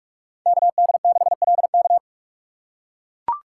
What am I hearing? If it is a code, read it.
Morse code, heard as a keyed tone, about 43 wpm, 700 Hz. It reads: KD6LK
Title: USA Call Signs - 5 Characters - 3x Character Spacing - 45wpm